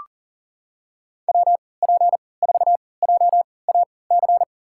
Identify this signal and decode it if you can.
Morse code, heard as a keyed tone, about 38 wpm, spaced as for 22 wpm, 700 Hz. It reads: WP4JAC
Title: USA Call Signs - 6 Characters - 3x Character Spacing - 40wpm